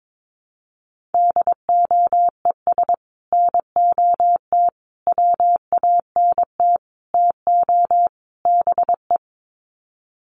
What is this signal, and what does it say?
Morse code, heard as a keyed tone, about 22 wpm, 700 Hz. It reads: DOES NOT WANT TO BE
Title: Sets of 5 Words Encore 22wpm